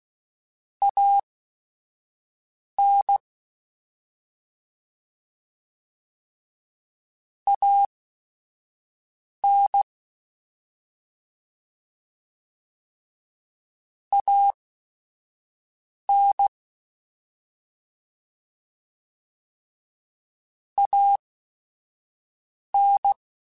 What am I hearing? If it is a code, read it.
Morse code, heard as a keyed tone, about 16 wpm, spaced as for 5 wpm, 800 Hz. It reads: AN AN AN AN
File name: Short Runs / Short Runs-mp3s / ShortContrasts @5